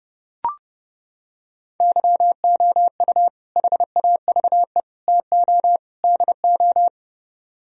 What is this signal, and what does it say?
Morse code, heard as a keyed tone, about 30 wpm, 700 Hz. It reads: YOU HAVE TO DO